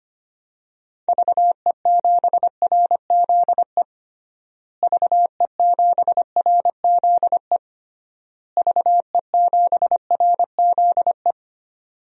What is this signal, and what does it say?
Morse code, heard as a keyed tone, about 25 wpm, 700 Hz. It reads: VE7RZE VE7RZE VE7RZE
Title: DXCC-100 Call Signs - 6 Characters - Repeated 3x - 25wpm